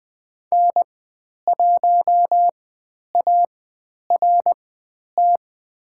Morse code, heard as a keyed tone, about 20 wpm, spaced as for 10 wpm, 700 Hz. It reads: N1ART
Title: 102 Lesson 17g - Letter u - Call Signs 20wpm